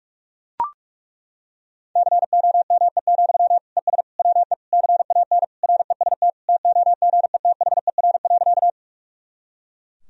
Morse code, heard as a keyed tone, about 41 wpm, 700 Hz. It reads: COME, ES WE CAN REST TOGETHER.
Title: Sentences from Top 500 Words and Abbreviations 45wpm